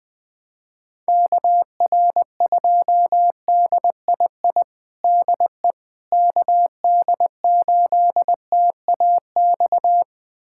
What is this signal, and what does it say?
Morse code, heard as a keyed tone, about 21 wpm, 700 Hz. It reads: KR2DII DE KD8TAX
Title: USA Callsign DE Callsign 20wpm